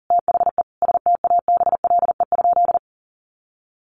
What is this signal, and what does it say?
Morse code, heard as a keyed tone, about 40 wpm, 700 Hz. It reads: THE STABLE?
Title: Sentences from Top 1500 Words 40wpm